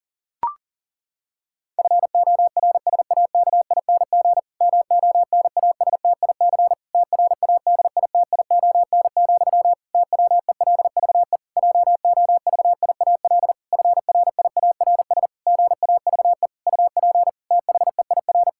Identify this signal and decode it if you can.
Morse code, heard as a keyed tone, about 40 wpm, 700 Hz. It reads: FORSAKING MONASTIC TRADITION, TWELVE JOVIAL FRIARS GAVE UP THEIR
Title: Pangram Challenge v2 - 40wpm